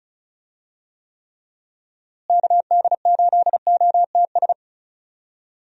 Morse code, heard as a keyed tone, about 34 wpm, 700 Hz. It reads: KD8OTS
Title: USA Call Signs - 6 Characters - Repeated 3x - 35wpm